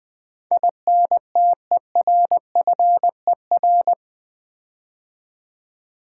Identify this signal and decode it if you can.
Morse code, heard as a keyed tone, about 20 wpm, 700 Hz. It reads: INTERFER
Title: Build A Word - 4 Letters At A Time - 12-Letter Words - 20wpm